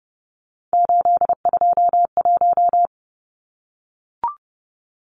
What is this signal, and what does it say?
Morse code, heard as a keyed tone, about 30 wpm, 700 Hz. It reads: 821
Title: Numbers - 3-Digit 30wpm